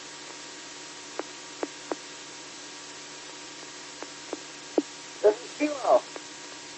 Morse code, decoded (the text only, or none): none